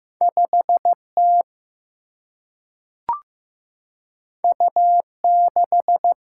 Morse code, heard as a keyed tone, about 15 wpm, 700 Hz. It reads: 5T U6
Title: Single-Letter Single-Number 15wpm